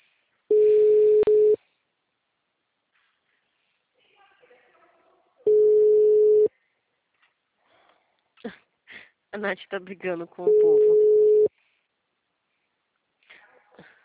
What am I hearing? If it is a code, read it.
Ringback tone (Europe, 425 Hz), with speech.